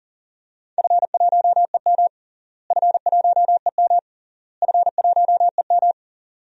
Morse code, heard as a keyed tone, about 37 wpm, 700 Hz. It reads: F1EM F1EM F1EM
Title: DXCC-100 Call Signs - 4 Characters - Repeated 3x - 40wpm